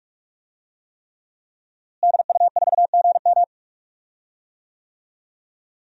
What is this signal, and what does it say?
Morse code, heard as a keyed tone, about 46 wpm, 700 Hz. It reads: DU4GM
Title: DXCC-100 Call Signs - 5 Characters - Repeated 3x - 45wpm